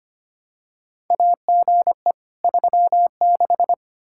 Morse code, heard as a keyed tone, about 25 wpm, 700 Hz. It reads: AGE 36
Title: Practice QSO - Exchange 3 - More Info - 25wpm